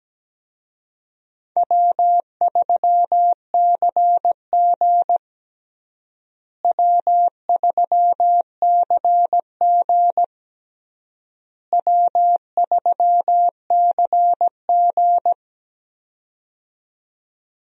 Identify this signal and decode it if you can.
Morse code, heard as a keyed tone, about 17 wpm, 700 Hz. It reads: W3CG W3CG W3CG
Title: USA Call Signs - 4 Characters - Repeated 3x - 17wpm